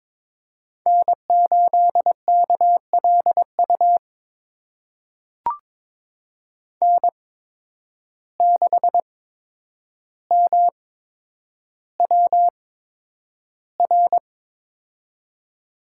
Morse code, heard as a keyed tone, about 21 wpm, spaced as for 9 wpm, 700 Hz. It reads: N8KLU N6MWR
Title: USA Call Signs - 5 Characters - 8x Character Spacing - 22wpm